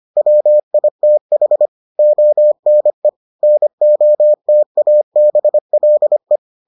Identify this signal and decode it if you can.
Morse code, heard as a keyed tone, about 25 wpm, 600 Hz. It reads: WITH ONE NOTABLE